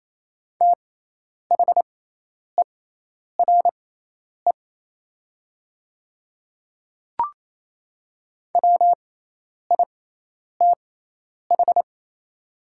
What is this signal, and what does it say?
Morse code, heard as a keyed tone, about 28 wpm, spaced as for 10 wpm, 700 Hz. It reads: THERE WITH
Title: 6x Character Spacing - Top 200 Words - 28wpm